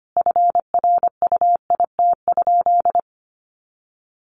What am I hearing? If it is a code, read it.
Morse code, heard as a keyed tone, about 25 wpm, 700 Hz. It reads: FRUIT?
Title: Sentences from Top 1500 Words 25wpm